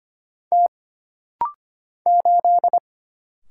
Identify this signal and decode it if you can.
Morse code, heard as a keyed tone, about 25 wpm, 700 Hz. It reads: T 8
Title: Single Letter-Number Alt - Rapid-Fire 25wpm